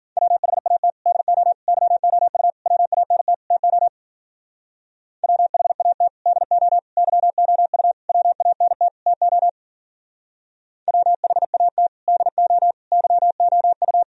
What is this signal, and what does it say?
Morse code, heard as a keyed tone, about 49 wpm, 700 Hz. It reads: WHAT DO YOU WANT TO WHAT DO YOU WANT TO WHAT DO YOU
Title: Speed-Racing - Sets of 5 Words 40wpm